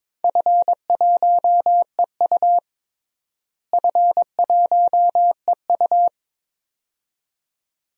Morse code, heard as a keyed tone, about 21 wpm, 700 Hz. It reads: F1EU F1EU
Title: DXCC-100 Call Signs - 4 Characters - Repeated 2x - 22wpm